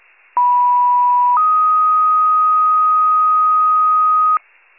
Two-tone page, heard as a one-second tone, then a three-second tone, about 980 then 1250 Hz. It is heard from 0.4 to 4.4 s.
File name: pager / 73 RESCUE / wcalert